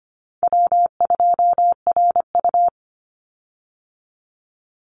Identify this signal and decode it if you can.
Morse code, heard as a keyed tone, about 25 wpm, 700 Hz. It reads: W2RU
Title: Call Signs Repeated 3x - 25wpm